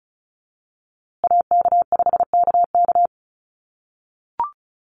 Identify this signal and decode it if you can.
Morse code, heard as a keyed tone, about 34 wpm, 700 Hz. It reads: AK5KK
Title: USA Call Signs Repeated 2x 35wpm